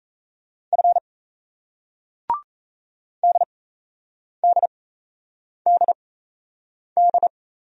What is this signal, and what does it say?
Morse code, heard as a keyed tone, about 38 wpm, spaced as for 8 wpm, 700 Hz. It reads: F DDDD